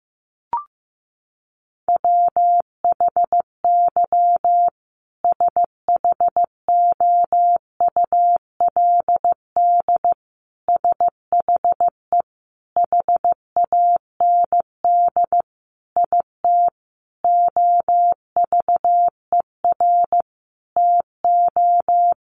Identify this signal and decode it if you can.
Morse code, heard as a keyed tone, about 15 wpm, 700 Hz. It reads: WHY SHOULD SHE HAND IT OVER TO
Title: Sentences from Top 900 Words 15wpm